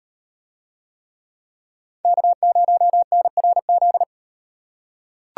Morse code, heard as a keyed tone, about 36 wpm, 700 Hz. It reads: K0NRZ